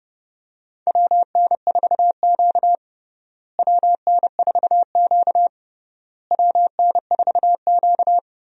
Morse code, heard as a keyed tone, about 30 wpm, 700 Hz. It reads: WN4Q WN4Q WN4Q
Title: USA Call Signs - 4 Characters - Repeated 3x - 30wpm